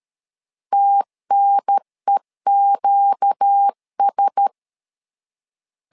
Morse code, heard as a keyed tone, about 13 wpm, 800 Hz. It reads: TNEQS
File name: LEZIONI / 4 Q1S 12wpm 1802